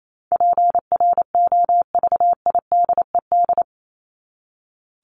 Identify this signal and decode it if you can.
Morse code, heard as a keyed tone, about 28 wpm, 700 Hz. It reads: PROVIDED
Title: Top 1751-2000 Words 28wpm